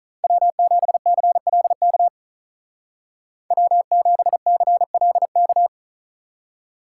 Morse code, heard as a keyed tone, about 43 wpm, 700 Hz. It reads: W7CLK W7CLK
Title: Speed Racing - USA 1x3 Call Signs 35wpm